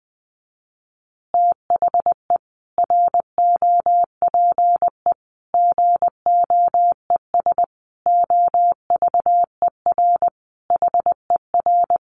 Morse code, heard as a keyed tone, about 20 wpm, 700 Hz. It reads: THE ROPE GOES OVER HER
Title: Easy Phrases 20wpm